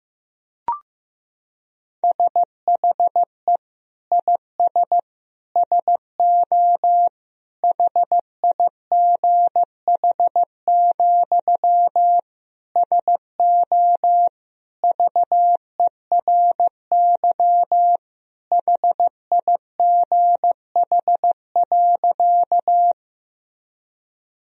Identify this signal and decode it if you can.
Morse code, heard as a keyed tone, about 15 wpm, 700 Hz. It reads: SHE IS SO HIGH, SO VERY HIGH.
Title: Sentences from Top 1000 Words - Review Part 1 - 15wpm